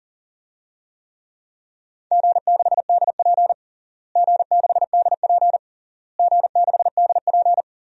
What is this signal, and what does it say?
Morse code, heard as a keyed tone, about 40 wpm, 700 Hz. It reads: G6DP G6DP G6DP
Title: DXCC-100 Call Signs - 4 Characters - Repeated 3x - 40wpm